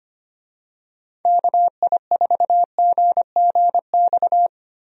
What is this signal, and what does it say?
Morse code, heard as a keyed tone, about 25 wpm, 700 Hz. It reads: KI4GGX
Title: USA Call Signs - 6 Characters - Repeated 3x - 25wpm